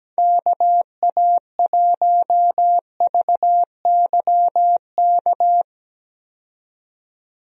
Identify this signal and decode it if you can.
Morse code, heard as a keyed tone, about 17 wpm, 700 Hz. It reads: KA1VYK
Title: USA Call Signs - 6 Characters - 17wpm